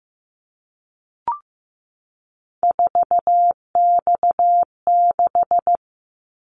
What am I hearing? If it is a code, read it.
Morse code, heard as a keyed tone, about 15 wpm, 700 Hz. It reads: 4X6
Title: Callsign Prefixes for Top 50 DXCC - 15wpm